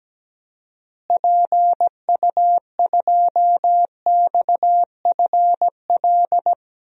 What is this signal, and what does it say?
Morse code, heard as a keyed tone, about 17 wpm, 700 Hz. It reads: PU2XFL
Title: DXCC-100 Call Signs - 6 Characters - Repeated 2x - 17wpm